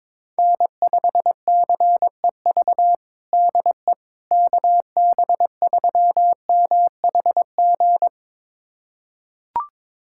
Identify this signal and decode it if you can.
Morse code, heard as a keyed tone, about 22 wpm, 700 Hz. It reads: N5CEV DE KB3MHG
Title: USA Callsign DE Callsign 22wpm